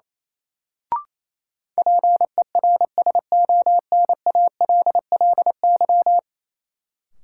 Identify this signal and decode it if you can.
Morse code, heard as a keyed tone, about 28 wpm, 700 Hz. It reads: PERSONALLY
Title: Sentences from Top 1400 Words - New Word Drill - 28wpm